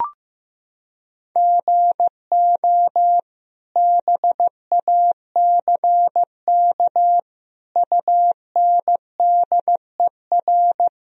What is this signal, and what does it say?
Morse code, heard as a keyed tone, about 16 wpm, 700 Hz. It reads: GO BACK UNDER
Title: Sentences from Top 1000 Words - Review Part 1 - 15wpm